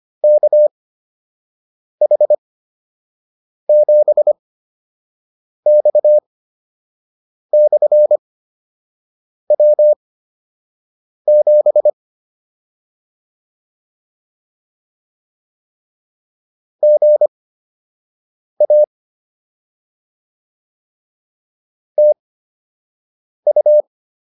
Morse code, heard as a keyed tone, about 25 wpm, spaced as for 6 wpm, 600 Hz. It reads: KH7X/W7 GA TU